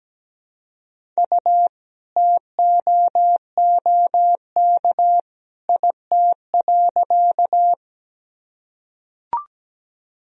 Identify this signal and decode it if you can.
Morse code, heard as a keyed tone, about 17 wpm, 700 Hz. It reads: U TOOK IT.
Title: Sentences from Top 300 Words and Abbreviations 17wpm